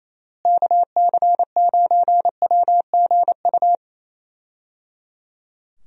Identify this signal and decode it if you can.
Morse code, heard as a keyed tone, about 28 wpm, 700 Hz. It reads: KC9WGU
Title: USA Call Signs - 6 Characters - Repeated 3x - 28wpm